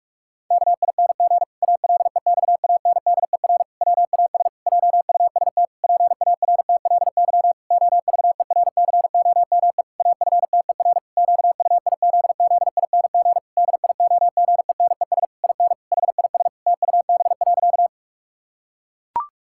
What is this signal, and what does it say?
Morse code, heard as a keyed tone, about 46 wpm, 700 Hz. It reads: KING ALEXANDER WAS JUST PARTLY OVERCOME AFTER QUIZZING DIOGENES IN HIS TUB.
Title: Pangram Challenge v2 - 45wpm